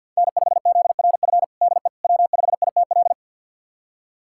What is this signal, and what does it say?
Morse code, heard as a keyed tone, about 48 wpm, 700 Hz. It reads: N5ZRF DE W5ITL
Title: USA Callsign DE Callsign - 5 Characters - 50wpm